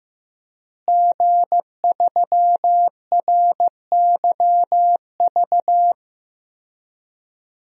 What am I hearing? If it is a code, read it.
Morse code, heard as a keyed tone, about 15 wpm, 700 Hz. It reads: G3RYV
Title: DXCC-100 Call Signs - 5 Characters - Repeated 2x - 15wpm